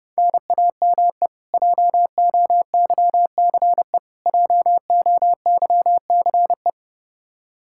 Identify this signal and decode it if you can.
Morse code, heard as a keyed tone, about 32 wpm, 700 Hz. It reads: NAME JOYCE JOYCE
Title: Practice QSO - Exchange 1 - Beginning - 30wpm